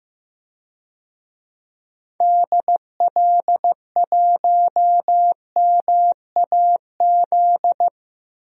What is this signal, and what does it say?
Morse code, heard as a keyed tone, about 15 wpm, 700 Hz. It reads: DL1MAZ
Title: DXCC-100 Call Signs - 6 Characters - 15wpm